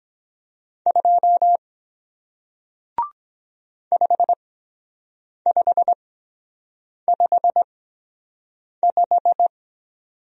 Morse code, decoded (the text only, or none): 2 5555